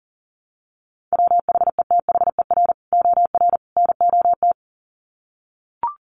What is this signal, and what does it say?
Morse code, heard as a keyed tone, about 40 wpm, 700 Hz. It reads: WHETHER OR NOT
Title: Sets of 3 Words 40wpm